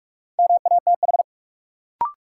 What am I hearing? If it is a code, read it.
Morse code, heard as a keyed tone, about 46 wpm, 700 Hz. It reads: MATH
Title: Sentences from Top 1000 Words - Review Part 6 - New Word Drill - 45wpm